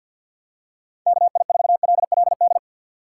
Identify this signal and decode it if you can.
Morse code, heard as a keyed tone, about 50 wpm, 700 Hz. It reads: KI4LLD
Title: USA Call Signs - 3x Character Spacing - 50wpm